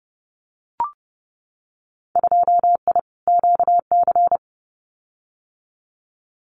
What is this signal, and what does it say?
Morse code, heard as a keyed tone, about 30 wpm, 700 Hz. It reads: 2I QC
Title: Winter Field Day Exchanges 30wpm